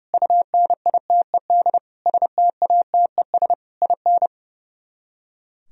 Morse code, heard as a keyed tone, about 38 wpm, 690 Hz. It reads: UNITED STATES IN